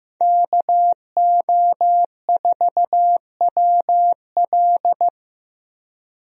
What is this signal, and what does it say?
Morse code, heard as a keyed tone, about 15 wpm, 700 Hz. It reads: KO4WL
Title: USA Call Signs - 5 Characters - 15wpm